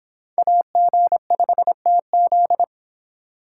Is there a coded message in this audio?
Yes: Morse code from 0.4 to 2.6 s, heard as a keyed tone, about 26 wpm, 700 Hz. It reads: AG5TZ